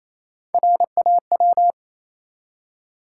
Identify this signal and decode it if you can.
Morse code, heard as a keyed tone, about 28 wpm, 700 Hz. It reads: RAW
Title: Three Letter Words - 28wpm